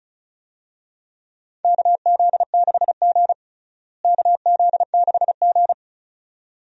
Morse code, heard as a keyed tone, about 34 wpm, 700 Hz. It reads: KZ6G KZ6G
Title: USA Call Signs - 4 Characters - Repeated 2x - 35wpm